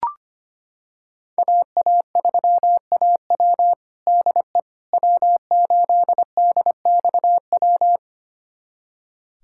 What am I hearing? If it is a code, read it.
Morse code, heard as a keyed tone, about 25 wpm, 700 Hz. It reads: AA3AW DE W8DXW
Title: USA Callsign DE Callsign - 5 Characters - 25wpm